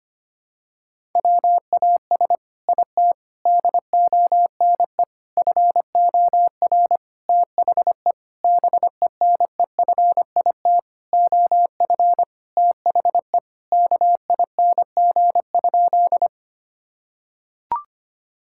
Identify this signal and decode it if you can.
Morse code, heard as a keyed tone, about 25 wpm, 700 Hz. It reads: WAS IT DONE FOR THE BENEFIT OF THE KING?
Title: Sentences from Top 1000 Words - Review Part 6 - 25wpm